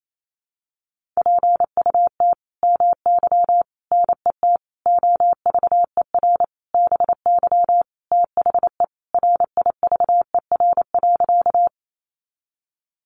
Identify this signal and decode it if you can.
Morse code, heard as a keyed tone, about 28 wpm, 700 Hz. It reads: PUT MY NET OVER BY THE RIVER.
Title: Sentences from Top 1400 Words 28wpm